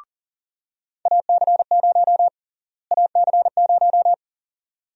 Morse code, heard as a keyed tone, about 40 wpm, 700 Hz. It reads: AC0 AC0
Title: USA Call Sign Prefixes - Repeated 2x - 40wpm